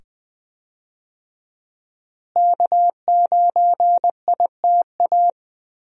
Morse code, heard as a keyed tone, about 20 wpm, 700 Hz. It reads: K9ITA
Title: USA Call Signs - 5 Characters - Repeated 2x - 20wpm